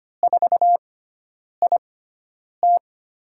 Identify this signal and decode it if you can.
Morse code, heard as a keyed tone, about 25 wpm, spaced as for 9 wpm, 700 Hz. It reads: 4IT
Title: USA Call Signs - 6x Character Spacing - 25wpm